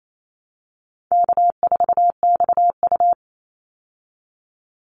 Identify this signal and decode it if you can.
Morse code, heard as a keyed tone, about 28 wpm, 700 Hz. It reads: K4XU
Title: Easy Call Signs - 28wpm